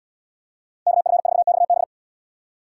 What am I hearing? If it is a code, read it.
Morse code, heard as a keyed tone, about 70 wpm, 690 Hz. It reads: KF4XB